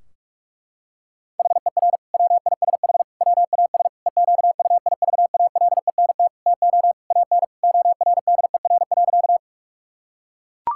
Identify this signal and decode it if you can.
Morse code, heard as a keyed tone, about 42 wpm, 700 Hz. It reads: HER WISH WAS EQUIVALENT TO AN ORDER.
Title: Sentences from Top 1600 Words 45wpm